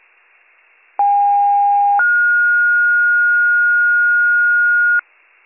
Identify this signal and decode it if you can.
Two-tone page, heard as a one-second tone, then a three-second tone, about 810 then 1430 Hz, with noise. It is heard from 1.0 to 5.0 s.